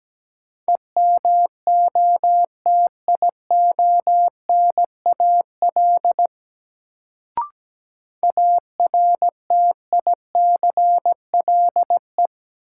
Morse code, heard as a keyed tone, about 17 wpm, 700 Hz. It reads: EMOTIONAL ARTICLE
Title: Sentences from Top 1000 Words - Review Part 5 - New Word Drill - 17wpm